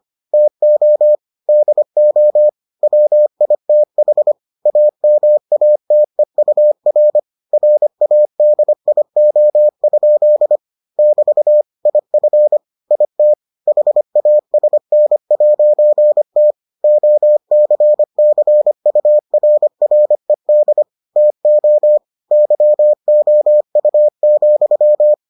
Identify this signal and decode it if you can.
Morse code, heard as a keyed tone, about 25 wpm, 600 Hz. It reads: TO DO WITH AMATEUR RADIO? <BT> IF IT HASN'T OCCURRED TO YOU,